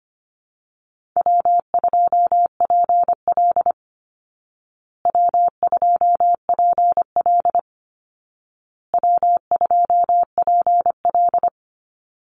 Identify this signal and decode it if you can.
Morse code, heard as a keyed tone, about 25 wpm, 700 Hz. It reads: W2PL W2PL W2PL
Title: Easy Call Signs Repeated 3x - 25wpm